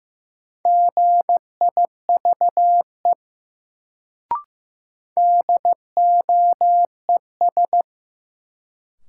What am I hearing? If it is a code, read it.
Morse code, heard as a keyed tone, about 15 wpm, 700 Hz. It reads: GIVE DOES
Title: Sets of 2 Words - New Word Drill - 15wpm